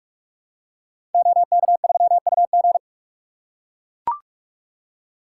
Morse code, heard as a keyed tone, about 44 wpm, 700 Hz. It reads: OK3UG
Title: Code Groups - 5 Letter:Numbers - 45wpm